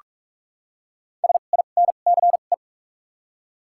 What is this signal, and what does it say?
Morse code, heard as a keyed tone, about 44 wpm, spaced as for 32 wpm, 700 Hz. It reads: SINCE
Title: 2x Character Spacing - Top 300 Words - 45wpm